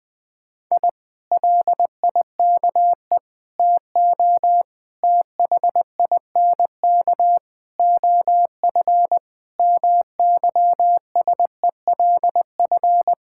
Morse code, heard as a keyed tone, about 20 wpm, 700 Hz. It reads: I LIKE TO THINK OF MYSELF
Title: Sets of 8 Words - 20wpm